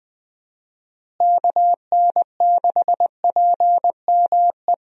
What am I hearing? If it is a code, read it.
Morse code, heard as a keyed tone, about 19 wpm, 700 Hz. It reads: KN6PME
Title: USA Call Signs - 6 Characters - Repeated 2x - 20wpm